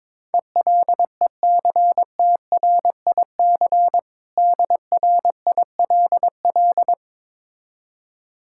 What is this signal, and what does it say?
Morse code, heard as a keyed tone, about 22 wpm, 700 Hz. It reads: ELECTRIC DRILL
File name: Difficult Words with Hints - 22wpm